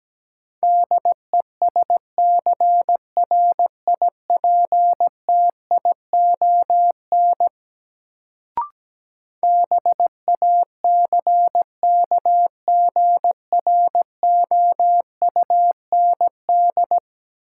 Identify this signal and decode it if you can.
Morse code, heard as a keyed tone, about 17 wpm, 700 Hz. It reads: DESCRIPTION BACKGROUND
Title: Sentences from Top 1300 Words - New Word Drill - 17wpm